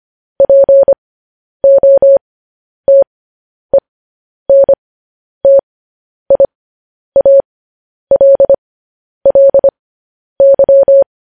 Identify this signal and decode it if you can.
Morse code, heard as a keyed tone, about 25 wpm, spaced as for 10 wpm, 550 Hz. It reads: POTENTIALLY
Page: news-headlines-in-morse-code-at-10-wpm